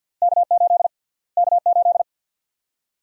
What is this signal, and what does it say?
Morse code, heard as a keyed tone, about 50 wpm, 700 Hz. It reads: K8 K8
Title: USA Call Sign Prefixes - Repeated 2x - 50wpm